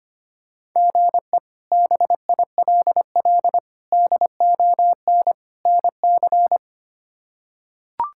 Morse code, heard as a keyed tone, about 25 wpm, 700 Hz. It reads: GE BILL DON NC
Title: K1USN SST QSO Practice - 25wpm